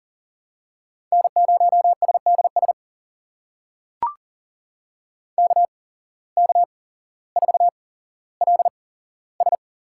Morse code, heard as a keyed tone, about 38 wpm, spaced as for 17 wpm, 700 Hz. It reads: N0SDS KK4LS
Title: USA Call Signs - 5 Characters - 8x Character Spacing - 40wpm